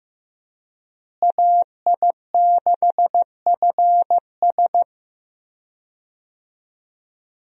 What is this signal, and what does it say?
Morse code, heard as a keyed tone, about 15 wpm, 700 Hz. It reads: AI6FS
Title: USA Call Signs - 5 Characters - Repeated 3x - 15wpm